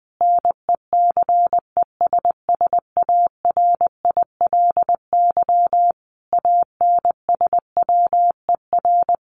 Morse code, heard as a keyed tone, about 20 wpm, 700 Hz. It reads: NECESSARILY ANSWER
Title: Sentences from Top 1400 Words 20wpm